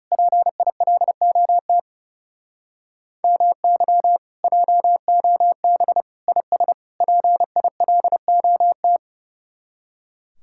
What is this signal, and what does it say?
Morse code, heard as a keyed tone, about 38 wpm, 700 Hz. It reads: PILOT MY JOB IS PILOT